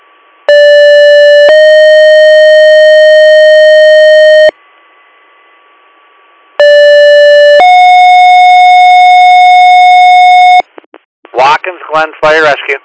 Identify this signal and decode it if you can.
Two-tone page, heard as a one-second tone, then a three-second tone, about 580 then 620 Hz, about 580 then 730 Hz, with speech. It is heard from 0.5 to 10.6 s.